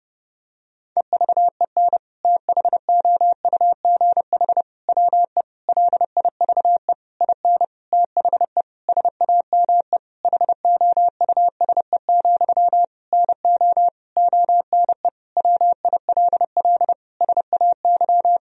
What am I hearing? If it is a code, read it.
Morse code, heard as a keyed tone, about 30 wpm, 700 Hz. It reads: EVEN THOUGH WE LIVE IN THE SAME HOUSE, NO ONE WILL SAY
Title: Sentences from Top 1000 Words - Review Part 2 - 30wpm